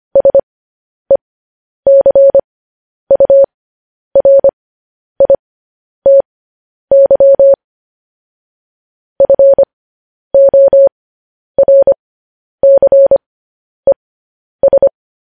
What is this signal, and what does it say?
Morse code, heard as a keyed tone, about 25 wpm, spaced as for 10 wpm, 550 Hz. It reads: SECURITY FORCES